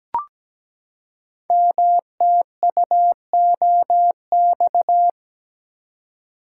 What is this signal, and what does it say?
Morse code, heard as a keyed tone, about 17 wpm, 700 Hz. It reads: MTUOX
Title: Code Groups - 5 Letters - 17wpm